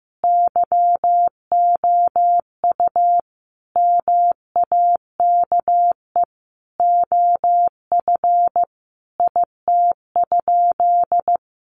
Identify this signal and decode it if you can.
Morse code, heard as a keyed tone, about 15 wpm, 700 Hz. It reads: YOU MAKE OF IT?
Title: Sentences from Top 100 Words 15wpm